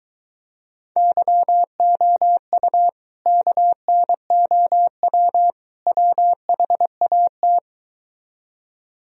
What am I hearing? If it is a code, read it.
Morse code, heard as a keyed tone, about 23 wpm, 700 Hz. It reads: YOU KNOW WHAT